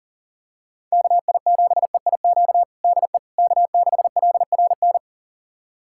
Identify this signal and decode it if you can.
Morse code, heard as a keyed tone, about 40 wpm, 700 Hz. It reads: KI7EIQ DE K6LRN
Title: USA Callsign DE Callsign 40wpm